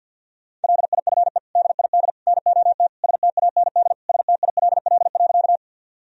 Morse code, heard as a keyed tone, about 47 wpm, 700 Hz. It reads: LIFE DID NOT STAND STILL.
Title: Sentences from Top 1000 Words - Review Part 1 - 50wpm